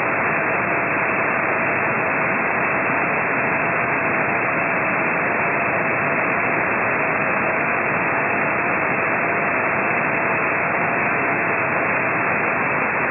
FT8.
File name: ft8-vs-cw / 35-to-150 FT8 -14